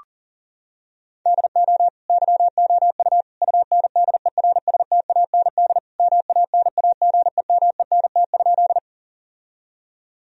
Morse code, heard as a keyed tone, about 40 wpm, 700 Hz. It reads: DO YOU UNDERSTAND MANAGEMENT?
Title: Sentences from Top 1000 Words - Review Part 4 - 40wpm